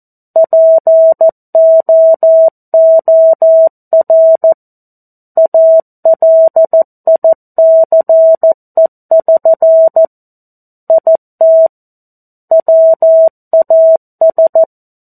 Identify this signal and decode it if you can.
Morse code, heard as a keyed tone, about 14 wpm, 650 Hz. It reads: POOR ALICE<SN> IT WAS
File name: 02-Chapter-02 14 WPM